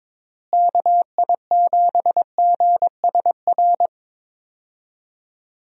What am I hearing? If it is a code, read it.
Morse code, heard as a keyed tone, about 22 wpm, 700 Hz. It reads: KI7GSR
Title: USA Call Signs - 6 Characters - Repeated 3x - 22wpm